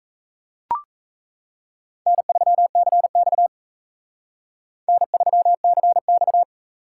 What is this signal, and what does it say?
Morse code, heard as a keyed tone, about 34 wpm, 700 Hz. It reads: N3CX N3CX
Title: Speed Racing - USA 1x2 Call Signs 28wpm